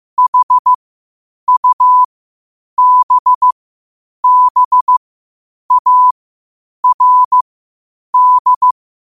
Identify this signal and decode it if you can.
Morse code, heard as a keyed tone, about 15 wpm, spaced as for 8 wpm, 1000 Hz. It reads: HUBBARD
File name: qotdfiles / 8wpm